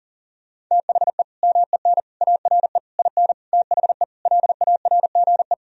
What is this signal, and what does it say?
Morse code, heard as a keyed tone, about 40 wpm, 700 Hz. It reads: THE MEN ARE IN THE LARGE
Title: Sentences from Top 1000 Words - Review Part 1 - 40wpm